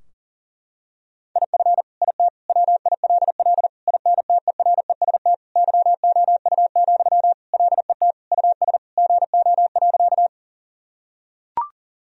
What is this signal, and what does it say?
Morse code, heard as a keyed tone, about 40 wpm, 700 Hz. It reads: IF IT WILL INTEREST YOU, LET US GO.
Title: Sentences from Top 1000 Words - Review Part 2 - 40wpm